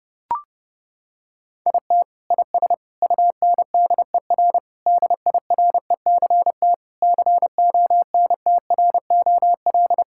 Morse code, heard as a keyed tone, about 32 wpm, 700 Hz. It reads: IT IS UNDER DIRECT CONTROL
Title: Sentences from Top 1000 Words - Review Part 5 - 30wpm